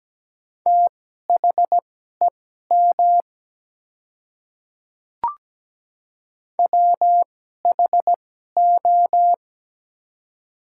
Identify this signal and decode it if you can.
Morse code, heard as a keyed tone, about 17 wpm, spaced as for 12 wpm, 690 Hz. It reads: THEM WHO
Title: 2x Character Spacing - Top 100 Words - 17wpm